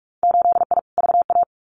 Morse code, heard as a keyed tone, about 46 wpm, 700 Hz. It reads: 8I VA
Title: Winter Field Day Exchanges Encore 45wpm